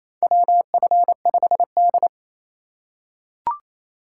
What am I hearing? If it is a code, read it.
Morse code, heard as a keyed tone, about 28 wpm, 700 Hz. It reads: WF5D
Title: USA Call Signs - 4 Characters - 3x Character Spacing - 28wpm